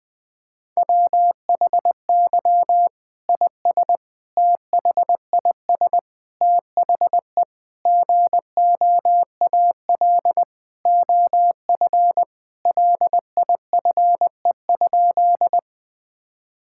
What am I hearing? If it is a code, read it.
Morse code, heard as a keyed tone, about 21 wpm, 700 Hz. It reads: WHY IS THIS THE GOAL OF LIFE?
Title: Sentences from Top 1000 Words - Review Part 5 - 20wpm